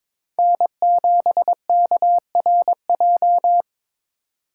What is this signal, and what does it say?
Morse code, heard as a keyed tone, about 25 wpm, 700 Hz. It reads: N7KRJ
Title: Speed Racing - USA 1x3 Call Signs 22wpm